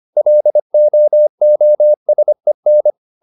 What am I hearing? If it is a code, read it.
Morse code, heard as a keyed tone, about 25 wpm, 600 Hz. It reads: LOOSEN
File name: the old man and the sea part 1 25wpm